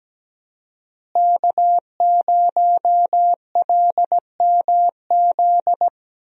Morse code, heard as a keyed tone, about 17 wpm, 700 Hz. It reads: K0LMZ
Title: USA Call Signs - 5 Characters - Repeated 3x - 17wpm